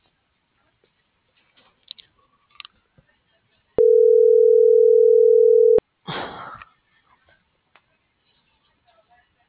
Ringback tone (North America), 440 + 480 Hz.